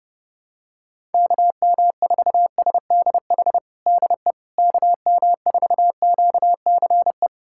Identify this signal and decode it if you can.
Morse code, heard as a keyed tone, about 30 wpm, 700 Hz. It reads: KM4SDH DE KM4QCE
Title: USA Callsign DE Callsign 30wpm